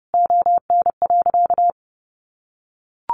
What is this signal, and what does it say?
Morse code, heard as a keyed tone, about 30 wpm, 700 Hz. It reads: ON.
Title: Sentences from Top 100 Words 30wpm